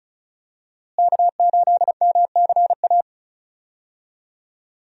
Morse code, heard as a keyed tone, about 36 wpm, 700 Hz. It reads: K8MCA
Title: USA Call Signs - 5 Characters - Repeated 2x - 35wpm